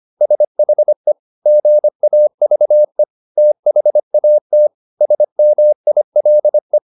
Morse code, heard as a keyed tone, about 25 wpm, 600 Hz. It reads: SHE GAVE THAT SMILE